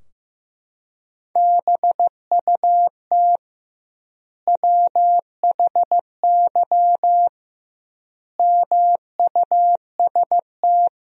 Morse code, heard as a keyed tone, about 16 wpm, 700 Hz. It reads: BUT WHY MUST
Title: 2x Word-Spacing - Sentences from Top 400 Words - 15wpm